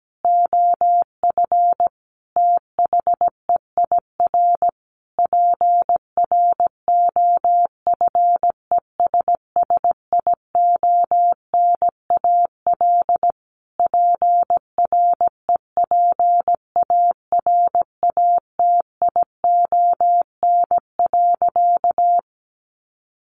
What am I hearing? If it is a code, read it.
Morse code, heard as a keyed tone, about 17 wpm, 700 Hz. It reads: OF THEIR PROFESSIONAL PREPARATION.
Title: Sentences from Top 1400 Words 17wpm